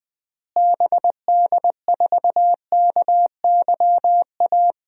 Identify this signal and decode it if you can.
Morse code, heard as a keyed tone, about 20 wpm, 700 Hz. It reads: BD4KYA
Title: DXCC-100 Call Signs - 6 Characters - Repeated 2x - 20wpm